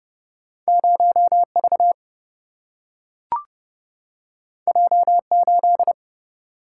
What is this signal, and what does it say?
Morse code, heard as a keyed tone, about 30 wpm, 700 Hz. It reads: 0V J8
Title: Single-Letter Single-Number 30wpm